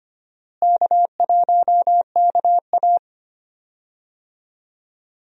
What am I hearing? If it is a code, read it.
Morse code, heard as a keyed tone, about 25 wpm, 700 Hz. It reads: K1KA
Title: USA Call Signs - 4 Characters - Repeated 2x - 25wpm